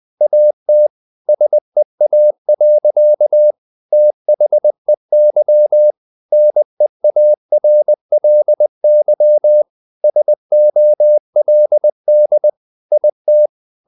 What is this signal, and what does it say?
Morse code, heard as a keyed tone, about 20 wpm, 600 Hz. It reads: AT SEA. THEY NEARLY SOLD IT
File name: the old man and the sea part 2 20wpm